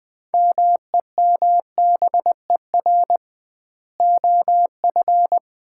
Morse code, heard as a keyed tone, about 21 wpm, 700 Hz. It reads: MEMBER OF
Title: 2x Word-Spacing - Sets of 3 Words - 20wpm